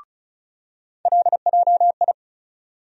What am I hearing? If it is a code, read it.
Morse code, heard as a keyed tone, about 36 wpm, 700 Hz. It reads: LJI
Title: USA Call Sign Postfixes - 3 Letters 35wpm